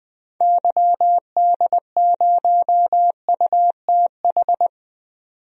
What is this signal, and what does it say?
Morse code, heard as a keyed tone, about 20 wpm, 700 Hz. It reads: YD0UTH
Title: DXCC-100 Call Signs - 6 Characters - 20wpm